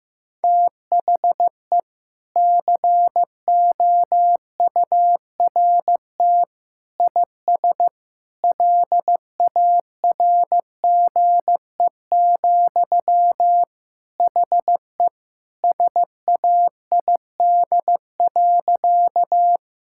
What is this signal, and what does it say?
Morse code, heard as a keyed tone, about 15 wpm, 700 Hz. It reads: THE COURT IS LARGE, HE SAID.
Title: Sentences from Top 1600 Words 15wpm